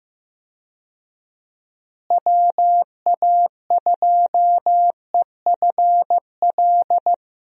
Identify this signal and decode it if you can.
Morse code, heard as a keyed tone, about 15 wpm, 700 Hz. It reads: WA2EFL
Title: USA Call Signs - 6 Characters - Repeated 3x - 15wpm